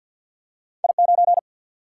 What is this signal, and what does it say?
Morse code, heard as a keyed tone, about 47 wpm, 700 Hz. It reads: I9
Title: Single-Letter Single-Number 50wpm